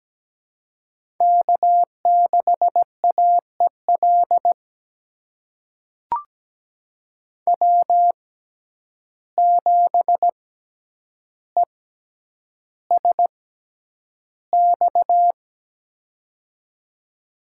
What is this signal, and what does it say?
Morse code, heard as a keyed tone, about 17 wpm, spaced as for 9 wpm, 700 Hz. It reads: K6AEL W7ESX
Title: USA Call Signs - 5 Characters - 6x Character Spacing - 17wpm